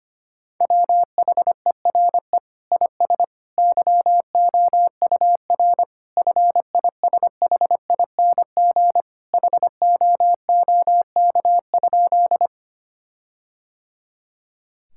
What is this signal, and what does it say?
Morse code, heard as a keyed tone, about 25 wpm, 700 Hz. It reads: WHERE IS YOUR FISHING HOOK?
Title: Sentences from Top 1600 Words 25wpm